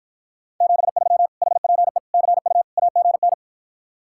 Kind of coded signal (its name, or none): Morse code